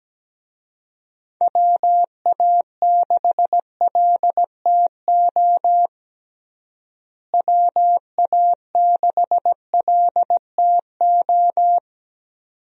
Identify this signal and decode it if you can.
Morse code, heard as a keyed tone, about 17 wpm, 700 Hz. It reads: WA6LTO WA6LTO
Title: USA Call Signs - 6 Characters - Repeated 2x - 17wpm